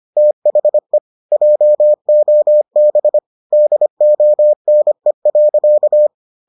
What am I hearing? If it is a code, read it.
Morse code, heard as a keyed tone, about 25 wpm, 600 Hz. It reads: THE JOB DONE.